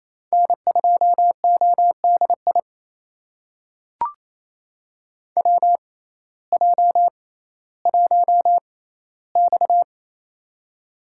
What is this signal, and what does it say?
Morse code, heard as a keyed tone, about 28 wpm, spaced as for 15 wpm, 700 Hz. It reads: N2ODI WJ1X
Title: USA Call Signs - 6x Character Spacing - 28wpm